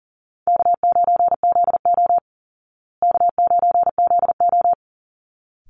Morse code, heard as a keyed tone, about 40 wpm, 700 Hz. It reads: K9ZO K9ZO
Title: Easy Call Signs Repeated 2x - 40wpm